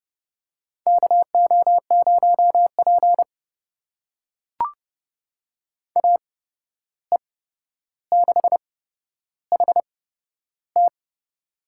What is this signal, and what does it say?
Morse code, heard as a keyed tone, about 30 wpm, spaced as for 12 wpm, 700 Hz. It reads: KO0P AE6HT
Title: USA Call Signs - 8x Character Spacing - 30wpm